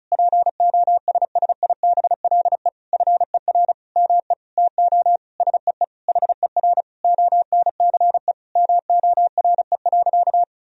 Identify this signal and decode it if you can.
Morse code, heard as a keyed tone, about 34 wpm, 700 Hz. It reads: POSSIBLE FER ME TO SEE HER ONCE MORE.
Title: Sentences from Top 500 Words and Abbreviations 35wpm